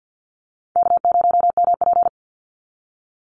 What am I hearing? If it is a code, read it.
Morse code, heard as a keyed tone, about 50 wpm, 700 Hz. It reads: K0MP
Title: NA SOTA Chaser Callsigns 50wpm